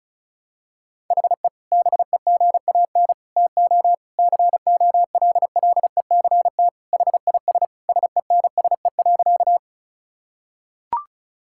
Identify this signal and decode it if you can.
Morse code, heard as a keyed tone, about 36 wpm, 700 Hz. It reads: HE BEGAN TO COLLECT HIS SENSE.
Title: Sentences from Top 1600 Words 35wpm